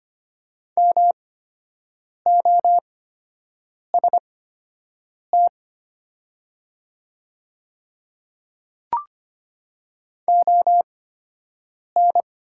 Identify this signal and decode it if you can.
Morse code, heard as a keyed tone, about 24 wpm, spaced as for 7 wpm, 700 Hz. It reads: MOST ON
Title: 8x Character Spacing - Top 200 Words - 25wpm